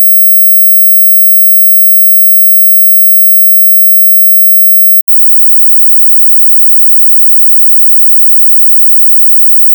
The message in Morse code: none